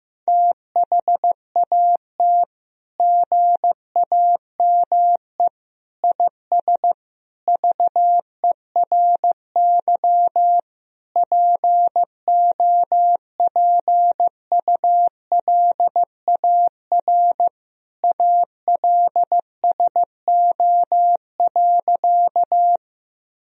Morse code, heard as a keyed tone, about 15 wpm, 700 Hz. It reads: THAT GAME IS VERY POPULAR ALSO.
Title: Sentences from Top 1000 Words - Review Part 3 - 15wpm